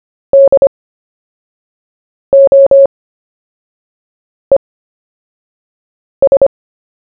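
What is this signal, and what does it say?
Morse code, heard as a keyed tone, about 25 wpm, spaced as for 5 wpm, 550 Hz. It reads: DOES
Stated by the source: CW story-telling practice, Short Story 1 sent at 5 wpm, by VK4JE, Queensland, Australia.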